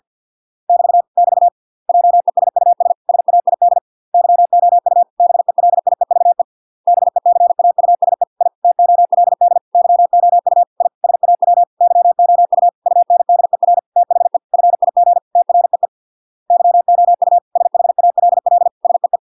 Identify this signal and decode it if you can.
Morse code, heard as a keyed tone, about 50 wpm, 700 Hz. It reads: <BT> <BT> JESUS SAID YOU BELIEVE BECAUSE I TOLD YOU I SAW YOU UNDER THE FIG TREE YOU SHALL SEE